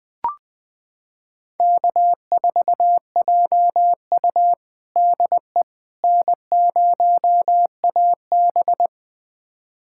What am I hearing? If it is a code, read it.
Morse code, heard as a keyed tone, about 20 wpm, 700 Hz. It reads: K4JU DE N0AB
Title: USA Callsign DE Callsign - 4 Characters - 20wpm